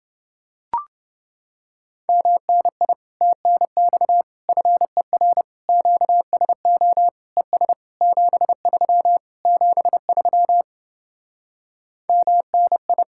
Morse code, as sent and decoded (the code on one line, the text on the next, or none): -- -. .. / - -. -..- / ..-. .-. / --.- ... --- / . ... / --... ...-- / --... ...-- / -- -. ..
MNI TNX FER QSO ES 73 73 MNI